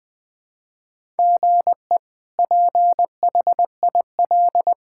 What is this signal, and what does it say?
Morse code, heard as a keyed tone, about 20 wpm, 700 Hz. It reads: GE PHIL
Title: K1USN SST QSO Practice - 20wpm